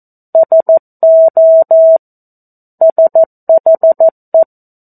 Morse code, heard as a keyed tone, about 14 wpm, 650 Hz. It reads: SO SHE
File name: 02-Chapter-02 14 WPM